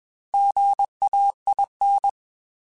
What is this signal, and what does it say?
Morse code, heard as a keyed tone, about 21 wpm, 800 Hz. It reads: GAIN
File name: Technical Subjects / Technical Subjects-mp3s / Sw Pwr Sup @20